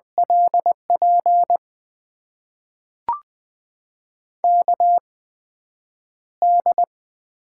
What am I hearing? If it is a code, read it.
Morse code, heard as a keyed tone, about 20 wpm, spaced as for 9 wpm, 700 Hz. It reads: LP KD
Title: USA Call Signs - 6 Characters - 8x Character Spacing - 20wpm